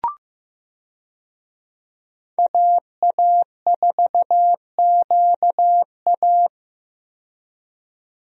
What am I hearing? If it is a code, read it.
Morse code, heard as a keyed tone, about 15 wpm, 700 Hz. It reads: AA4QA
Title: USA Call Signs - 5 Characters - 15wpm